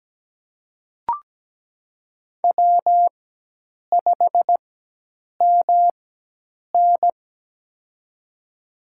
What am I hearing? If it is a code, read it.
Morse code, heard as a keyed tone, about 17 wpm, spaced as for 8 wpm, 700 Hz. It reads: W5MN
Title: USA Call Signs - 4 Characters - 4x Character Spacing - 17wpm